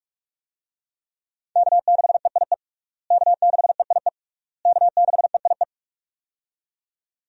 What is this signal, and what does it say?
Morse code, heard as a keyed tone, about 41 wpm, 700 Hz. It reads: K6EIE K6EIE K6EIE
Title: USA Call Signs - 5 Characters - Repeated 3x - 45wpm